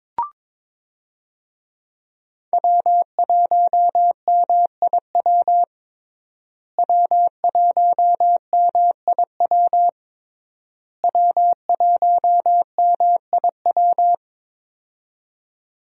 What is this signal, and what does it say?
Morse code, heard as a keyed tone, about 22 wpm, 700 Hz. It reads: W1MIW W1MIW W1MIW
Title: USA Call Signs - 5 Characters - Repeated 3x - 22wpm